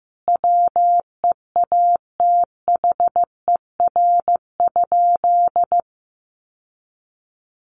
Morse code, heard as a keyed tone, about 15 wpm, 700 Hz. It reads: WEATHER?
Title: Sentences from Top 800 Words 15wpm